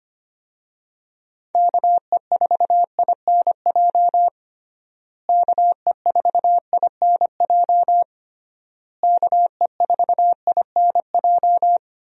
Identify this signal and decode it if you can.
Morse code, heard as a keyed tone, about 25 wpm, 700 Hz. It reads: KE4INJ KE4INJ KE4INJ
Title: USA Call Signs - 6 Characters - Repeated 3x - 25wpm